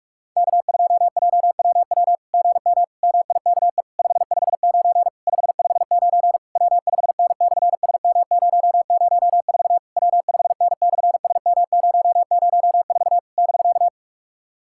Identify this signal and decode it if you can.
Morse code, heard as a keyed tone, about 44 wpm, 690 Hz. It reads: K2JWW GM MIKE 559 559 W5N/SM004 W5N/SM004 <BK>
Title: SOTA QSO Practice 45wpm